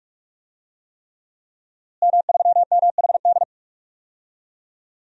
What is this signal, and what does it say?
Morse code, heard as a keyed tone, about 46 wpm, 700 Hz. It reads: M3MHD
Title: DXCC-100 Call Signs - 5 Characters - Repeated 3x - 45wpm